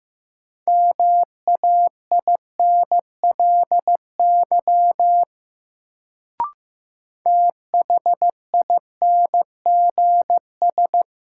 Morse code, heard as a keyed tone, about 15 wpm, 700 Hz. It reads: MAINLY THINGS
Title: Sentences from Top 1400 Words - New Word Drill Alt - 15wpm